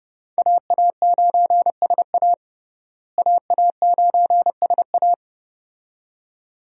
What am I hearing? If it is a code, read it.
Morse code, heard as a keyed tone, about 30 wpm, 700 Hz. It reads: AA9SA AA9SA
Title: USA Call Signs - 5 Characters - Repeated 2x - 30wpm